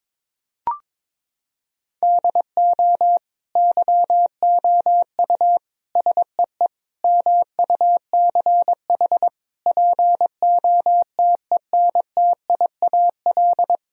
Morse code, heard as a keyed tone, about 22 wpm, 700 Hz. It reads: DO YOU SEE MUCH POTENTIAL
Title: Sentences from Top 1000 Words - Review Part 4 - 22wpm